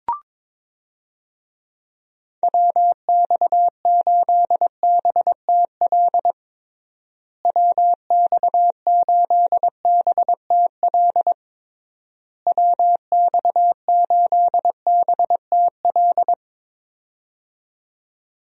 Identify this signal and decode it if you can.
Morse code, heard as a keyed tone, about 22 wpm, 700 Hz. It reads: WX8BTL WX8BTL WX8BTL
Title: USA Call Signs - 6 Characters - Repeated 3x - 22wpm